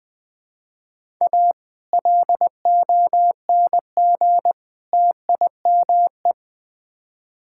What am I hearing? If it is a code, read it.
Morse code, heard as a keyed tone, about 20 wpm, 700 Hz. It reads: A LONG TIME
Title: Build A Phrase - 3-Word Phrases - 20wpm